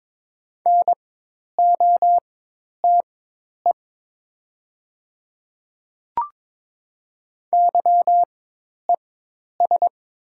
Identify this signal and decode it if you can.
Morse code, heard as a keyed tone, about 22 wpm, spaced as for 10 wpm, 700 Hz. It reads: NOTE YES
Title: 4x Character Spacing - Top 500 Words - 22wpm